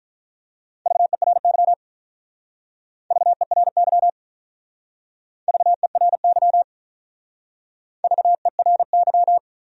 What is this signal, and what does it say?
Morse code, heard as a keyed tone, about 43 wpm, 700 Hz. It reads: VERY VERY VERY VERY
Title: Speed-Racing Top 200 Words 35wpm